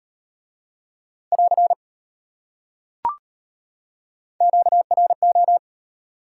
Morse code, heard as a keyed tone, about 33 wpm, 700 Hz. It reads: <AR> QRO